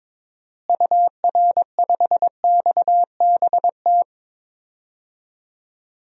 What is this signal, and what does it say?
Morse code, heard as a keyed tone, about 22 wpm, 700 Hz. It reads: UR5XBT
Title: DXCC-100 Call Signs - 6 Characters - Repeated 3x - 22wpm